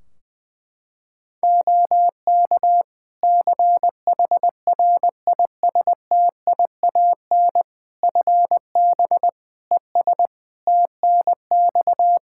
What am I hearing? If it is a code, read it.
Morse code, heard as a keyed tone, about 20 wpm, 700 Hz. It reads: OK CHRISTIAN FB ES TNX
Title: Practice QSO - Exchange 2 - Info - 20wpm